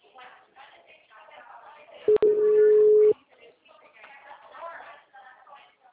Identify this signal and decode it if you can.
Ringback tone (Europe, 425 Hz), with speech.
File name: out- -2005- - - .1582